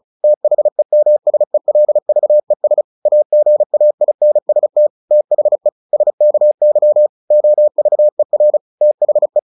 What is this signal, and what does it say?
Morse code, heard as a keyed tone, about 34 wpm, 600 Hz. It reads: THEMSELVES AGAINST THE SKY OVER THE
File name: the old man and the sea / the old man and the sea 1 35wpm